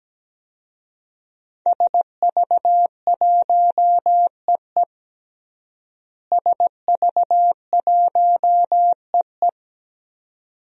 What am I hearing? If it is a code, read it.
Morse code, heard as a keyed tone, about 17 wpm, 700 Hz. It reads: SV1EE SV1EE
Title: DXCC-100 Call Signs - 5 Characters - Repeated 2x - 17wpm